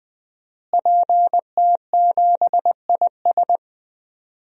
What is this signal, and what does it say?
Morse code, heard as a keyed tone, about 20 wpm, 700 Hz. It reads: PT7IS
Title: DXCC-100 Call Signs - 5 Characters - Repeated 2x - 20wpm